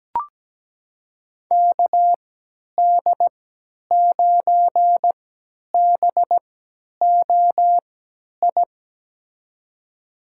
Morse code, heard as a keyed tone, about 17 wpm, spaced as for 10 wpm, 700 Hz. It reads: KD9BOI
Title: USA Call Signs - 6 Characters - 3x Character Spacing - 17wpm